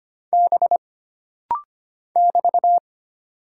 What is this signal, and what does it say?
Morse code, heard as a keyed tone, about 25 wpm, 700 Hz. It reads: B <BT>
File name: Single Character (Letter:Number:Punctuation:Prosign) - Rapid-Fire - 25wpm